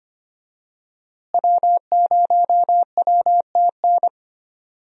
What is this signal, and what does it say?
Morse code, heard as a keyed tone, about 25 wpm, 700 Hz. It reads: W0WTN
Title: USA Call Signs - 5 Characters - Repeated 3x - 25wpm